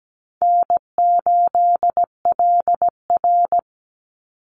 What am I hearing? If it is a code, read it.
Morse code, heard as a keyed tone, about 17 wpm, 700 Hz. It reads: N8LR
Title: Easy Call Signs Repeated 3x - 17wpm